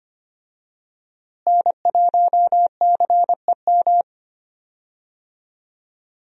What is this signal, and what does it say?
Morse code, heard as a keyed tone, about 25 wpm, 700 Hz. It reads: N1CEM
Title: USA Call Signs - 5 Characters - Repeated 2x - 25wpm